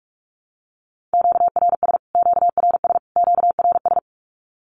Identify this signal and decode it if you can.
Morse code, heard as a keyed tone, about 44 wpm, 700 Hz. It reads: QRS QRS QRS
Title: Q-Codes Repeated 3x 45wpm